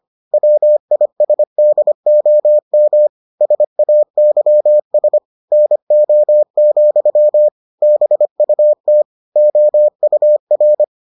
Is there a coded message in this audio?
Yes: Morse code from 0.3 to 10.8 s, heard as a keyed tone, about 25 wpm, 600 Hz. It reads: WISDOM SAYS NO, BUT OUR